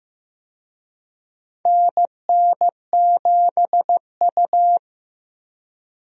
Morse code, heard as a keyed tone, about 15 wpm, 700 Hz. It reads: NN7U
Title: USA Call Signs - 4 Characters - 15wpm